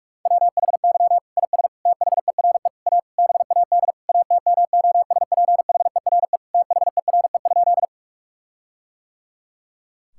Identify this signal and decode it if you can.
Morse code, heard as a keyed tone, about 46 wpm, 700 Hz. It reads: WHY IS THERE A BAD ATMOSPHERE THERE?
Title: Sentences from Top 1600 Words 45wpm